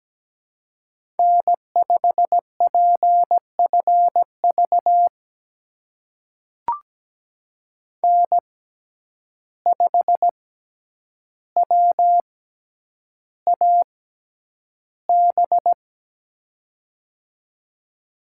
Morse code, heard as a keyed tone, about 17 wpm, spaced as for 9 wpm, 700 Hz. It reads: N5PFV N5WAB
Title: USA Call Signs - 5 Characters - 6x Character Spacing - 17wpm